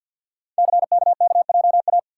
Morse code, heard as a keyed tone, about 47 wpm, 700 Hz. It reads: CKKJA
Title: Code Groups - 5 Letters - 50wpm